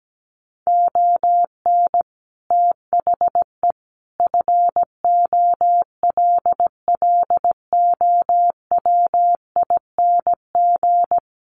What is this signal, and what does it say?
Morse code, heard as a keyed tone, about 17 wpm, 700 Hz. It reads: ON THE FOLLOWING
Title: Sentences from Top 1400 Words 17wpm